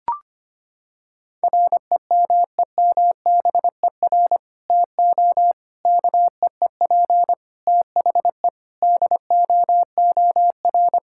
Morse code, heard as a keyed tone, about 25 wpm, 700 Hz. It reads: REMEMBER TO KEEP THE DOOR
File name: practice / 25wpm / Sentences with 3-Word Phrases - 25wpm